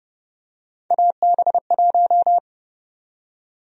Morse code, heard as a keyed tone, about 32 wpm, 700 Hz. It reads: AB1
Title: USA Call Sign Prefixes - 30wpm